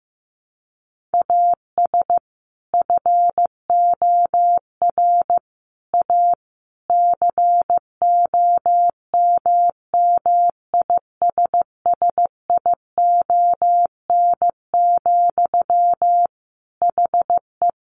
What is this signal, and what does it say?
Morse code, heard as a keyed tone, about 15 wpm, 700 Hz. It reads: AS FOR A COMMISSION, HE
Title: Sentences from Top 1400 Words 15wpm